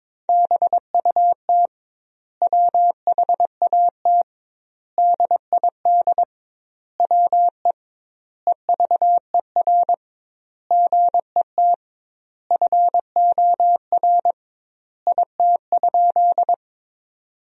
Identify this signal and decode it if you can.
Morse code, heard as a keyed tone, about 22 wpm, 700 Hz. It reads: BUT WHAT DID WE EVER GET FOR IT?
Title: 2x Word-Spacing - Sentences from Top 400 Words - 22wpm